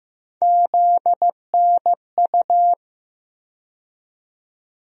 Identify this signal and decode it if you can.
Morse code, heard as a keyed tone, about 15 wpm, 700 Hz. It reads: ZNU
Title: USA Call Sign Postfixes - 3 Letters 15wpm